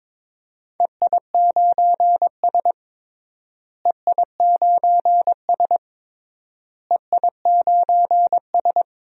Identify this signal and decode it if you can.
Morse code, heard as a keyed tone, about 22 wpm, 700 Hz. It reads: EI9S EI9S EI9S
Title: DXCC-100 Call Signs - 4 Characters - Repeated 3x - 22wpm